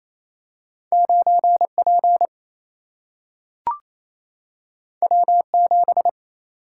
Morse code, heard as a keyed tone, about 28 wpm, 700 Hz. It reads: 9P W7
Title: Single-Letter Single-Number 28wpm